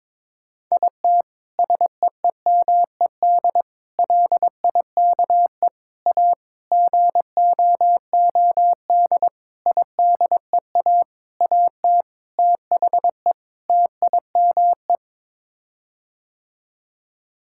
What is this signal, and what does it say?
Morse code, heard as a keyed tone, about 22 wpm, 700 Hz. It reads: IT SEEMED LIKE A GOOD IDEA AT THE TIME
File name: Sets of 9 Words - 22wpm